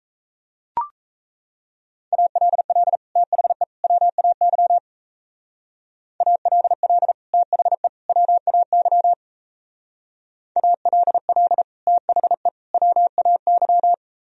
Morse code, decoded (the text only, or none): ALL THE WAY ALL THE WAY ALL THE WAY